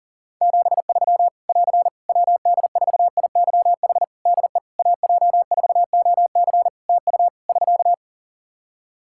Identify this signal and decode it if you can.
Morse code, heard as a keyed tone, about 44 wpm, 700 Hz. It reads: 73 <AR> WD4IYH DE AJ4OC TU <SK>
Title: Practice QSO - Exchange 4 - Ending - 40wpm